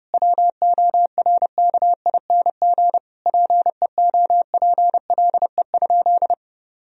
Morse code, heard as a keyed tone, about 32 wpm, 700 Hz. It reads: WORKING PEOPLE?
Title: Sentences from Top 1000 Words - Review Part 4 - 30wpm